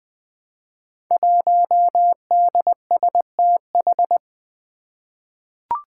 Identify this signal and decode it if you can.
Morse code, heard as a keyed tone, about 20 wpm, 700 Hz. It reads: 1DSTH
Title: Five Challenging Letter-Number Combinations 20wpm